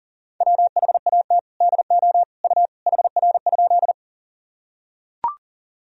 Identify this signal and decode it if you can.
Morse code, heard as a keyed tone, about 40 wpm, 700 Hz. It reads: WHAT DO U HR?
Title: Sentences from Top 300 Words and Abbreviations 40wpm